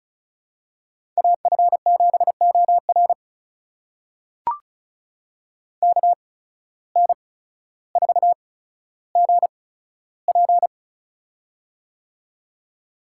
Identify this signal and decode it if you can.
Morse code, heard as a keyed tone, about 34 wpm, spaced as for 15 wpm, 700 Hz. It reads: AF7OR KN4GP
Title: USA Call Signs - 5 Characters - 8x Character Spacing - 35wpm